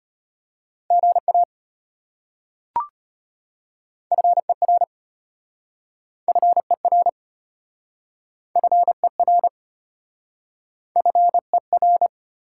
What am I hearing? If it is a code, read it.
Morse code, heard as a keyed tone, about 33 wpm, 700 Hz. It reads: GA FER FER FER FER